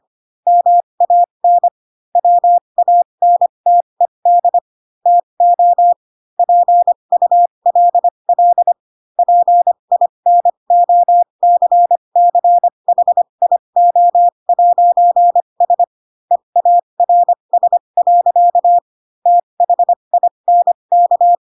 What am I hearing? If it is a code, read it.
Morse code, heard as a keyed tone, about 25 wpm, spaced as for 22 wpm, 700 Hz. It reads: MAN WANTED TO PULL PINOCCHIO'S EARS. THINK
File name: Pinocchio / Pinocchio ch3 18 25 ik1hge